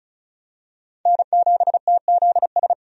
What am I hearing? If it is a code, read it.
Morse code, heard as a keyed tone, about 36 wpm, 700 Hz. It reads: N7TZS
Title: USA Call Signs - 5 Characters - 35wpm